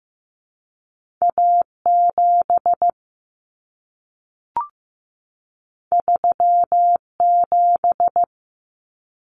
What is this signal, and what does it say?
Morse code, heard as a keyed tone, about 15 wpm, 700 Hz. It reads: A7 37
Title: Common Cut Numbers - 2-Digit 15wpm